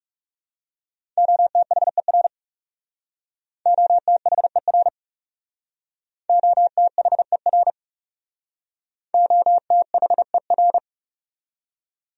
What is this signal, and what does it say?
Morse code, heard as a keyed tone, about 39 wpm, 700 Hz. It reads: OTHER OTHER OTHER OTHER